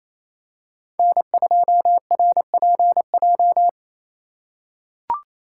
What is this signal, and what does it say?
Morse code, heard as a keyed tone, about 28 wpm, 700 Hz. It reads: N2RPJ
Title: USA Call Signs - 2x Character Spacing - 28wpm